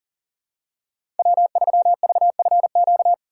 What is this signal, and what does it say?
Morse code, heard as a keyed tone, about 40 wpm, 700 Hz. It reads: W3VFQ
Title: USA Call Signs - 5 Characters - 3x Character Spacing - 40wpm